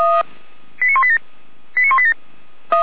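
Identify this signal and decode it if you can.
Five-tone selective call (ZVEI), heard as quick runs of five tones, tones of about 70 ms. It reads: 89187 89187
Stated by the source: ZVEI five-tone selective call 89191, received on SDR.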